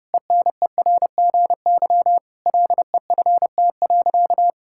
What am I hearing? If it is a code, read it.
Morse code, heard as a keyed tone, about 32 wpm, 700 Hz. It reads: ENERGY LEFT.
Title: Sentences from Top 1000 Words - Review Part 6 - 30wpm